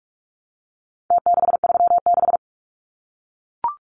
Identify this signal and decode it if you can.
Morse code, heard as a keyed tone, about 44 wpm, 700 Hz. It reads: T636
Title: Numbers - Contest Style - 4-Digit 45wpm